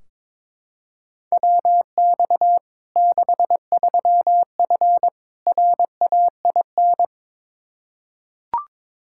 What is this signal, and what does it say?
Morse code, heard as a keyed tone, about 21 wpm, 700 Hz. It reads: WX 63F RAIN
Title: Practice QSO - Exchange 2 - Info - 22wpm